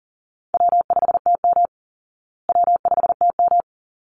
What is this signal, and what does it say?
Morse code, heard as a keyed tone, about 40 wpm, 700 Hz. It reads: W5TM W5TM
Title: Easy Call Signs Repeated 2x - 40wpm